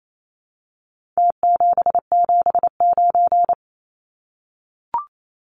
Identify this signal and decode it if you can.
Morse code, heard as a keyed tone, about 28 wpm, 700 Hz. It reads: T779
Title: Numbers - Contest Style - 4-Digit 28wpm